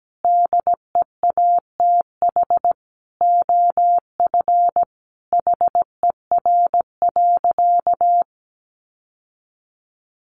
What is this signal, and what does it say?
Morse code, heard as a keyed tone, about 17 wpm, 700 Hz. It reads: DEATH OF HER.
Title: Sentences from Top 1100 Words 17wpm